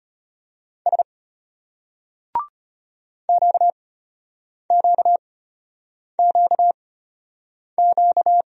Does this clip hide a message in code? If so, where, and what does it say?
Yes: Morse code from 0.9 to 8.4 s, heard as a keyed tone, about 34 wpm, spaced as for 8 wpm, 700 Hz. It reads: S QQQQ